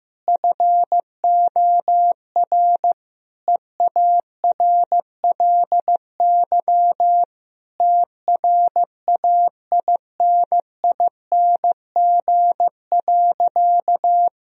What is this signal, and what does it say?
Morse code, heard as a keyed tone, about 15 wpm, 700 Hz. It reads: FOR EARLY TRAINING.
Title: Sentences from Top 1000 Words - Review Part 4 - 15wpm